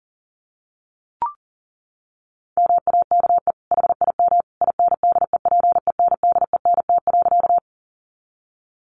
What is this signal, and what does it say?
Morse code, heard as a keyed tone, about 40 wpm, 700 Hz. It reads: MAKE HIM INDEPENDENT.
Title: Sentences from Top 900 Words 40wpm